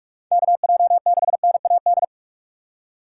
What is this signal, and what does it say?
Morse code, heard as a keyed tone, about 46 wpm, 700 Hz. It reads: KJ6NAD
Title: USA Call Signs - 4x Character Spacing - 45wpm